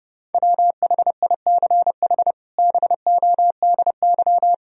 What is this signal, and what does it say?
Morse code, heard as a keyed tone, about 30 wpm, 700 Hz. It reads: WHICH BODY
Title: Sentences from Top 1000 Words - Review Part 2 - 30wpm